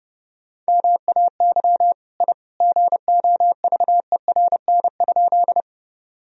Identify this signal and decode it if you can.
Morse code, heard as a keyed tone, about 30 wpm, 700 Hz. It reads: MAY I GOVERN?
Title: Sentences from Top 1000 Words - Review Part 2 - 30wpm